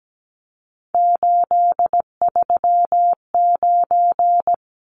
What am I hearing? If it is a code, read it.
Morse code, heard as a keyed tone, about 17 wpm, 700 Hz. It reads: 839
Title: Numbers - 3-Digit 17wpm